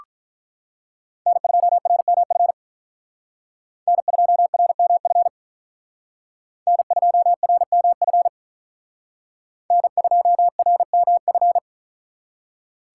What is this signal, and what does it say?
Morse code, heard as a keyed tone, about 43 wpm, 700 Hz. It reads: N2RMF N2RMF N2RMF N2RMF
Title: Speed Racing - USA Call Signs 35wpm